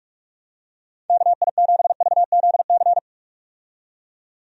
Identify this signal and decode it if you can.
Morse code, heard as a keyed tone, about 46 wpm, 700 Hz. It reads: KI7VZC
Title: USA Call Signs - 6 Characters - 45wpm